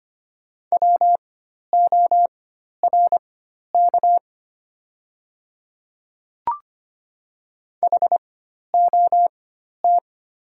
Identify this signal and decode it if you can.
Morse code, heard as a keyed tone, about 25 wpm, spaced as for 12 wpm, 700 Hz. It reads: WORK HOT
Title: 4x Character Spacing - Top 200 Words - 25wpm